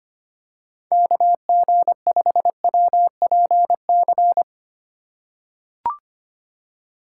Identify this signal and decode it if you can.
Morse code, heard as a keyed tone, about 25 wpm, 700 Hz. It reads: KG5WPC
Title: USA Call Signs - 2x Character Spacing - 25wpm